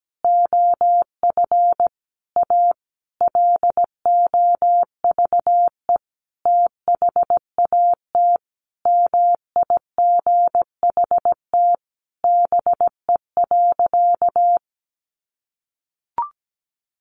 Morse code, heard as a keyed tone, about 17 wpm, 700 Hz. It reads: OF A LOVE THAT MIGHT BE.
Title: Sentences from Top 1500 Words 17wpm